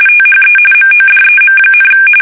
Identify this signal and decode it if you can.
RTTY, about 45.45 baud. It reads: THE MODEM,
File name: fsk850 / teletype-RTTY